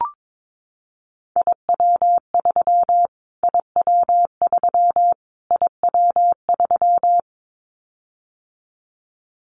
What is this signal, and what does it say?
Morse code, heard as a keyed tone, about 22 wpm, 700 Hz. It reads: IW3 IW3 IW3
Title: Callsign Prefixes for Top 25 DXCC - Repeated 3x - 22wpm